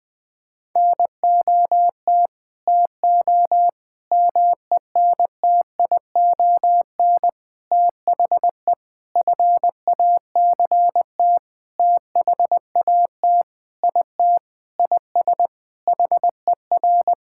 Morse code, heard as a keyed tone, about 19 wpm, 700 Hz. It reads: NOT TO MENTION THE FACT THAT IT IS HER
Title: Sentences with 6-Word Phrases - 20wpm